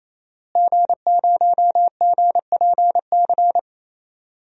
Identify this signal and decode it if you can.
Morse code, heard as a keyed tone, about 27 wpm, 700 Hz. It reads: G0GPC
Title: DXCC-100 Call Signs - 5 Characters - Repeated 3x - 28wpm